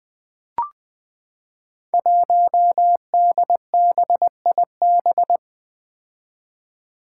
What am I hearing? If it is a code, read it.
Morse code, heard as a keyed tone, about 20 wpm, 700 Hz. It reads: 1DBIB
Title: Five Challenging Letter-Number Combinations 20wpm